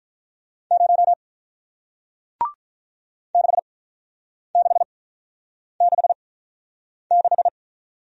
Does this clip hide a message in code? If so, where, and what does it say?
Yes: Morse code from 0.7 to 7.5 s, heard as a keyed tone, about 45 wpm, spaced as for 9 wpm, 700 Hz. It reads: 0 6666